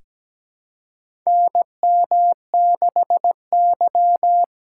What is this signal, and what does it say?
Morse code, heard as a keyed tone, about 17 wpm, 700 Hz. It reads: NM6Y
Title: USA Call Signs - 4 Characters - 6x Character Spacing - 17wpm